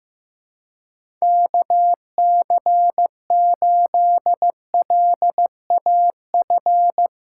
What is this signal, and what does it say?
Morse code, heard as a keyed tone, about 15 wpm, 700 Hz. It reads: KC8LAF
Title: USA Callsign DE Callsign - 6 Characters - 15wpm